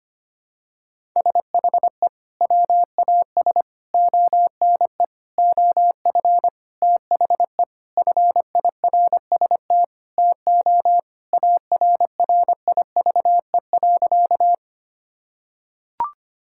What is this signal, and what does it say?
Morse code, heard as a keyed tone, about 25 wpm, 700 Hz. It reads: SHE WAS ONE OF THE FIRST TO ARRIVE.
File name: practice / 25wpm / Sentences with 4-Word Phrases - 25wpm